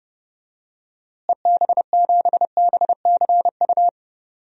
Morse code, heard as a keyed tone, about 32 wpm, 700 Hz. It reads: EB7BCU
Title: DXCC-100 Call Signs - 6 Characters - Repeated 3x - 30wpm